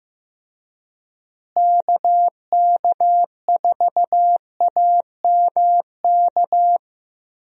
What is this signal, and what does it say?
Morse code, heard as a keyed tone, about 15 wpm, 700 Hz. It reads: KK4AMK
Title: USA Call Signs - 6 Characters - Repeated 3x - 15wpm